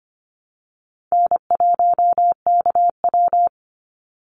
Morse code, heard as a keyed tone, about 25 wpm, 700 Hz. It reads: N1KW
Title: Call Signs Repeated 2x - 25wpm